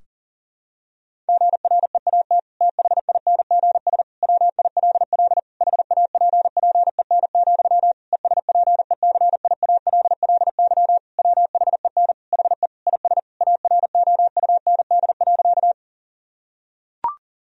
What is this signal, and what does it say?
Morse code, heard as a keyed tone, about 37 wpm, 700 Hz. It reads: GREAT THINGS WILL HAPPEN, ESPECIALLY WHEN HE IS AROUND.
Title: Sentences from Top 1000 Words - Review Part 4 - 40wpm